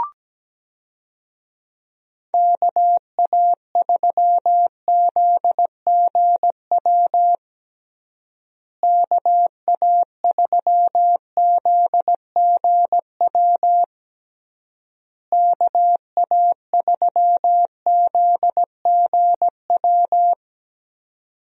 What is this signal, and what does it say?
Morse code, heard as a keyed tone, about 17 wpm, 700 Hz. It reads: KA3ZGW KA3ZGW KA3ZGW
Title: USA Call Signs - 6 Characters - Repeated 3x - 17wpm